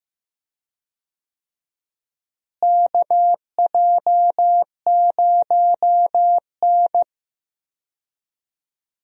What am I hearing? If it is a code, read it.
Morse code, heard as a keyed tone, about 15 wpm, 700 Hz. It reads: KJ0N
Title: USA Call Signs - 4 Characters - Repeated 3x - 15wpm